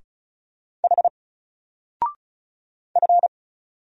Morse code, heard as a keyed tone, about 36 wpm, 700 Hz. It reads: H F
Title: Single Character (Letter:Number:Punctuation:Prosign) 35wpm